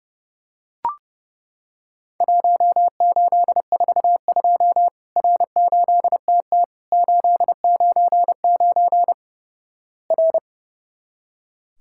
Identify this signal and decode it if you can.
Morse code, heard as a keyed tone, about 32 wpm, 700 Hz. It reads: 1842 R8TT 899 R
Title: Worked All Europe DX Contest - QTC Exchange - 30wpm